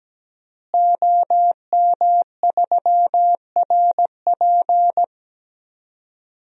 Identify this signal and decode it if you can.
Morse code, heard as a keyed tone, about 17 wpm, 700 Hz. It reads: OM3RP
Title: DXCC-100 Call Signs - 5 Characters - Repeated 3x - 17wpm